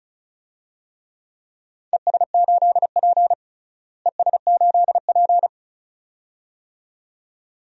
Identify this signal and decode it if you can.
Morse code, heard as a keyed tone, about 34 wpm, 700 Hz. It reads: ES8P ES8P
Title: DXCC-100 Call Signs - 4 Characters - Repeated 2x - 35wpm